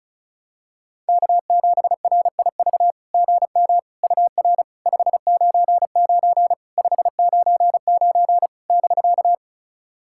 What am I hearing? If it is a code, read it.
Morse code, heard as a keyed tone, about 34 wpm, 690 Hz. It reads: K7RIV GM UR 599 599 <BK>
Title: SOTA QSO Practice 35wpm